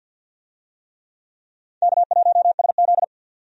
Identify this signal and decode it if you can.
Morse code, heard as a keyed tone, about 52 wpm, 700 Hz. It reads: K1SZ
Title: USA Call Signs - 4 Characters - 50wpm